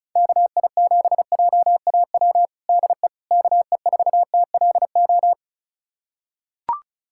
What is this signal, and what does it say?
Morse code, heard as a keyed tone, about 36 wpm, 690 Hz. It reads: KI7JAW DE KE4TLO
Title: USA Callsign DE Callsign 35wpm